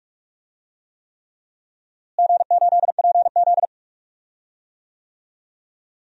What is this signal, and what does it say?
Morse code, heard as a keyed tone, about 44 wpm, 700 Hz. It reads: G8PZ
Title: DXCC-100 Call Signs - 4 Characters - Repeated 3x - 45wpm